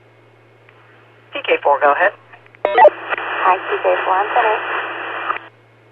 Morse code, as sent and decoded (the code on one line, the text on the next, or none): none